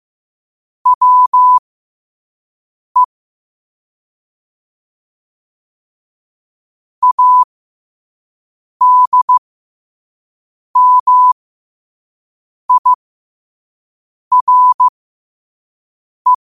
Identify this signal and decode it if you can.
Morse code, heard as a keyed tone, about 15 wpm, spaced as for 5 wpm, 1000 Hz. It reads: WE ADMIRE
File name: qotdfiles / 5wpm